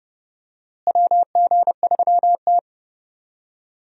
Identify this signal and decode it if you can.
Morse code, heard as a keyed tone, about 32 wpm, 700 Hz. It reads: WG3T
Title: USA Call Signs - 4 Characters - Repeated 3x - 30wpm